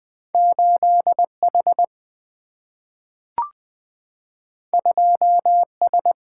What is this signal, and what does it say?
Morse code, heard as a keyed tone, about 20 wpm, 700 Hz. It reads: 8H 2S
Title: Single-Letter Single-Number 20wpm